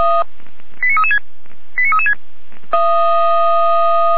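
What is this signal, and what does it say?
Five-tone selective call (ZVEI), heard as quick runs of five tones, tones of about 70 ms. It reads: 89336 89336
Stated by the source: ZVEI five-tone selective call received on an SDR, from the Feuerwehr Orlamünde fire brigade audio server.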